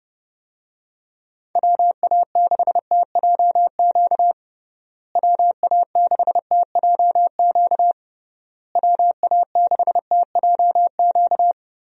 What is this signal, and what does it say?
Morse code, heard as a keyed tone, about 30 wpm, 700 Hz. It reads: WA6TJQ WA6TJQ WA6TJQ
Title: USA Call Signs - 6 Characters - Repeated 3x - 30wpm